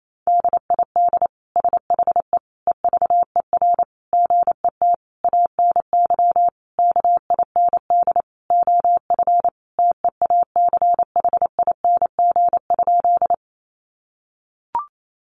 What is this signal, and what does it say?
Morse code, heard as a keyed tone, about 28 wpm, 700 Hz. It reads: DID SHE EVER GET ANY KIND OF TEACHING?
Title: Sentences from Top 1100 Words 28wpm